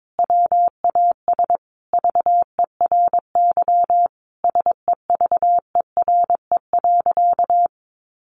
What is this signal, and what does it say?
Morse code, heard as a keyed tone, about 22 wpm, 700 Hz. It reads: WAS VERY SEVERE.
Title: Sentences from Top 1400 Words 22wpm